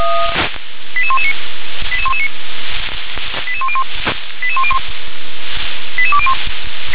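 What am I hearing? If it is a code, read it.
Five-tone selective call (ZVEI), heard as quick runs of five tones, tones of about 70 ms. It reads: 88119 88119 88101 88201
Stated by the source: Five-tone ZVEI selective call received by SDR.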